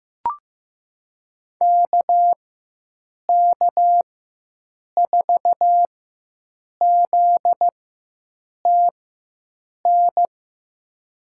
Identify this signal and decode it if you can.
Morse code, heard as a keyed tone, about 15 wpm, spaced as for 7 wpm, 700 Hz. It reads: KK4ZTN
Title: USA Call Signs - 4x Character Spacing - 15wpm